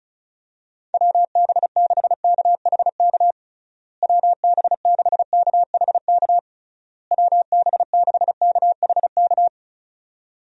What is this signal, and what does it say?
Morse code, heard as a keyed tone, about 34 wpm, 700 Hz. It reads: WB6KHK WB6KHK WB6KHK
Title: USA Call Signs - 6 Characters - Repeated 3x - 35wpm